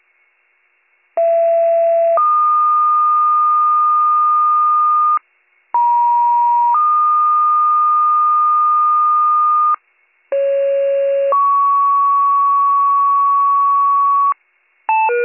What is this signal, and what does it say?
Two-tone page, heard as a one-second tone, then a three-second tone, about 670 then 1220 Hz, about 950 then 1220 Hz, about 550 then 1120 Hz. It is heard from 1.2 to 14.3 s.